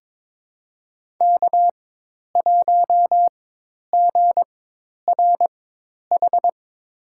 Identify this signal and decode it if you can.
Morse code, heard as a keyed tone, about 22 wpm, spaced as for 10 wpm, 700 Hz. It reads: K1GRH
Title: USA Call Signs - 5 Characters - 4x Character Spacing - 22wpm